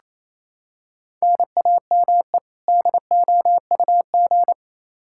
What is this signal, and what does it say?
Morse code, heard as a keyed tone, about 28 wpm, 700 Hz. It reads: NAME DOUG
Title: Practice QSO - Exchange 1 - Beginning - 28wpm